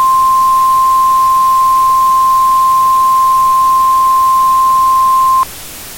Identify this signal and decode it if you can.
Emergency Alert System (SAME) tones (the 1050 Hz attention tone). It is heard from 1.2 to 5.5 s.